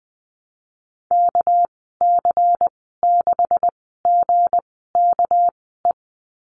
Morse code, heard as a keyed tone, about 20 wpm, 700 Hz. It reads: K C 6 G K E
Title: 200 Lesson 32g - Letter k - Call Signs 20wpm